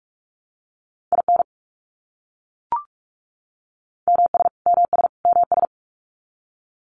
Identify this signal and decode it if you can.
Morse code, heard as a keyed tone, about 44 wpm, 700 Hz. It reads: IN MS MS MS
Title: US States Repeated 3x 45wpm